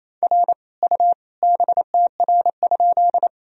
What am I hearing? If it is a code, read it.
Morse code, heard as a keyed tone, about 28 wpm, 700 Hz. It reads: R U BTR?
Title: Sentences from Top 400 Words and Abbreviations 28wpm